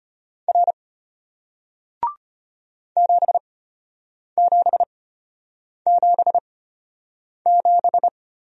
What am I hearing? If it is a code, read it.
Morse code, heard as a keyed tone, about 34 wpm, spaced as for 8 wpm, 700 Hz. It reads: R 7777